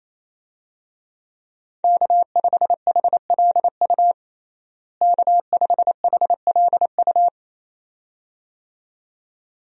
Morse code, heard as a keyed tone, about 28 wpm, 700 Hz. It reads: K5HLU K5HLU
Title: USA Call Signs - 5 Characters - Repeated 2x - 28wpm